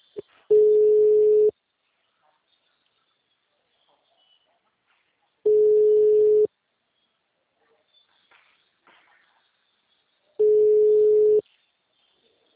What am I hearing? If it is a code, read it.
Ringback tone (Europe, 425 Hz).
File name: out- -2011- - - .0